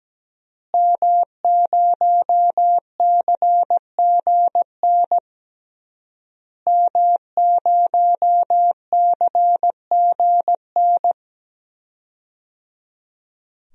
Morse code, heard as a keyed tone, about 17 wpm, 700 Hz. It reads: M0CGN M0CGN
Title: DXCC-100 Call Signs - 5 Characters - Repeated 3x - 17wpm